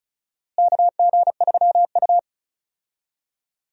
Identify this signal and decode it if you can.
Morse code, heard as a keyed tone, about 36 wpm, 700 Hz. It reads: KG3U
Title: USA Call Signs - 4 Characters - Repeated 2x - 35wpm